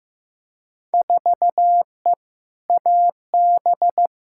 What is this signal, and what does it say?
Morse code, heard as a keyed tone, about 15 wpm, 700 Hz. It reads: 4E AB
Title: Field Day Exchanges Encore - 15wpm